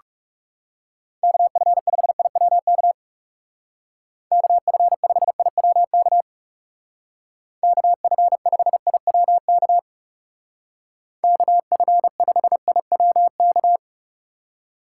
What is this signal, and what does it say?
Morse code, heard as a keyed tone, about 36 wpm, 700 Hz. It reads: KF5IWK KF5IWK KF5IWK KF5IWK
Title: Speed Racing - USA 2x3 Call Signs 30wpm